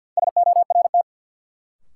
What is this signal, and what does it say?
Morse code, heard as a keyed tone, about 53 wpm, 700 Hz. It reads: SORT
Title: Sentences from Top 1000 Words - Review Part 3 - New Word Drill - 50wpm